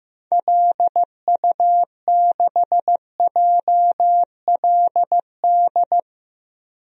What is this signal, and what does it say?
Morse code, heard as a keyed tone, about 15 wpm, 700 Hz. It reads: LU6JLD
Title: DXCC-100 Call Signs - 6 Characters - Repeated 3x - 15wpm